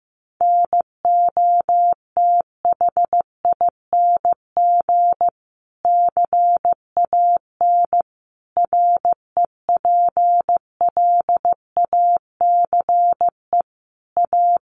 Morse code, heard as a keyed tone, about 15 wpm, 700 Hz. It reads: NOTHING CAN REPLACE A
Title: Sentences from Top 1400 Words 15wpm